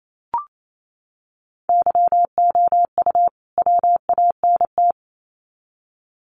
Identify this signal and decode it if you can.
Morse code, heard as a keyed tone, about 28 wpm, 700 Hz. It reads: YOU WANT
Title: Sets of 2 Words 28wpm